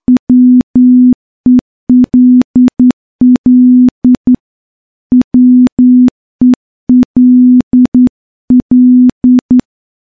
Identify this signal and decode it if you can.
Morse code, heard as a keyed tone, about 10 wpm, 250 Hz. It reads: WELL WELL
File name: morse-training-10WPM